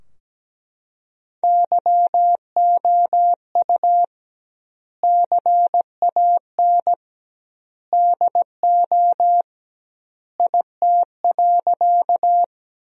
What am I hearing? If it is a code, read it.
Morse code, heard as a keyed tone, about 17 wpm, 700 Hz. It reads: YOU CAN DO IT.
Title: 2x Word-Spacing - Sentences from Top 100 Words - 17wpm